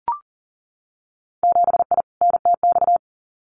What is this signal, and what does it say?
Morse code, heard as a keyed tone, about 42 wpm, 700 Hz. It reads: 7I NTX
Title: Winter Field Day Exchanges Encore 40wpm